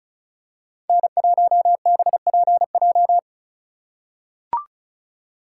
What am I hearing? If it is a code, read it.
Morse code, heard as a keyed tone, about 34 wpm, 700 Hz. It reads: N1BPJ
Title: USA Call Signs - 5 Characters - 8x Character Spacing - 35wpm